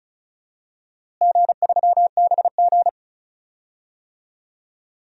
Morse code, heard as a keyed tone, about 36 wpm, 700 Hz. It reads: G3BG
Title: DXCC-100 Call Signs - 4 Characters - Repeated 2x - 35wpm